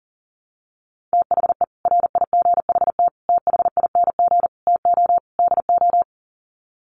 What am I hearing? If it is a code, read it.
Morse code, heard as a keyed tone, about 40 wpm, 700 Hz. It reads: THE RIGHT THING TO DO
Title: Sets of 5 Words 40wpm